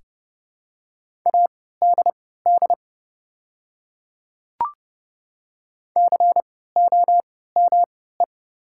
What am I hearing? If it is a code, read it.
Morse code, heard as a keyed tone, about 30 wpm, spaced as for 17 wpm, 700 Hz. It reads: ADD COME
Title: 3x Character Spacing - Top 200 Words - 30wpm